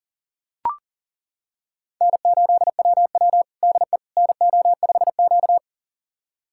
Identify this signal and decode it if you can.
Morse code, heard as a keyed tone, about 40 wpm, 700 Hz. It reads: N8WW DE NO5Q
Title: USA Callsign DE Callsign - 4 Characters - 40wpm